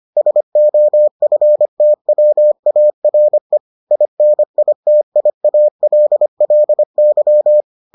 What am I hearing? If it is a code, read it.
Morse code, heard as a keyed tone, about 25 wpm, 600 Hz. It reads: SOFTWARE INITIALLY